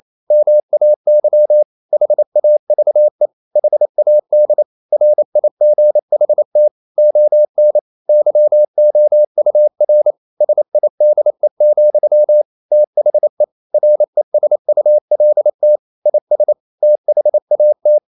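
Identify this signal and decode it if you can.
Morse code, heard as a keyed tone, about 28 wpm, 600 Hz. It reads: MAY HAVE HAD RIGHT ON YOUR SIDE, THE RESULT IS THAT
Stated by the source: Morse code practice at 28 wpm, text of Ken Follett's novel Never in English, from CW med Gnister.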